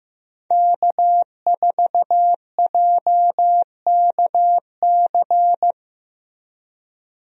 Morse code, heard as a keyed tone, about 16 wpm, 700 Hz. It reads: K4JKC
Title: USA Call Signs - 5 Characters - Repeated 3x - 15wpm